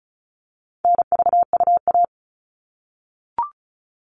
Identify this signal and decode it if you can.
Morse code, heard as a keyed tone, about 34 wpm, 700 Hz. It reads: NVUA
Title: All Cut Numbers - 4-Digit 35wpm